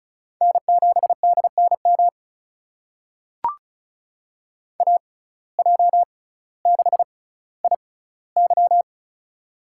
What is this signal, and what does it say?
Morse code, heard as a keyed tone, about 34 wpm, spaced as for 18 wpm, 700 Hz. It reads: N7DNM AJ6IY
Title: USA Call Signs - 5 Characters - 6x Character Spacing - 35wpm